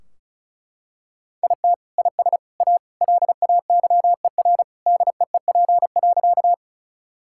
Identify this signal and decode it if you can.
Morse code, heard as a keyed tone, about 34 wpm, 700 Hz. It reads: IT IS A LAYER DEEP.
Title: Sentences from Top 1600 Words 35wpm